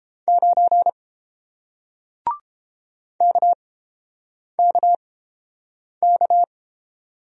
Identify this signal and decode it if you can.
Morse code, heard as a keyed tone, about 30 wpm, spaced as for 7 wpm, 700 Hz. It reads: 9 KKK